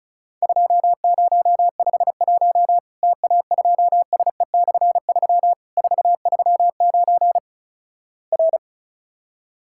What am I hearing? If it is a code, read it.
Morse code, heard as a keyed tone, about 36 wpm, 700 Hz. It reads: 2051 TA2SE/3 439 R
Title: Worked All Europe DX Contest - QTC Exchange - 35wpm